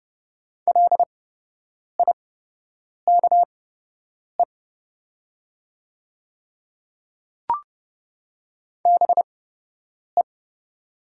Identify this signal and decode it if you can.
Morse code, heard as a keyed tone, about 30 wpm, spaced as for 8 wpm, 700 Hz. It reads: LIKE BE
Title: 8x Character Spacing - Top 100 Words - 30wpm